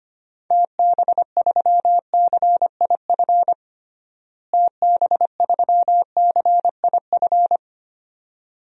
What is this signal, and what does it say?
Morse code, heard as a keyed tone, about 25 wpm, 700 Hz. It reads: TB3CIF TB3CIF
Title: DXCC-100 Call Signs - 6 Characters - Repeated 2x - 25wpm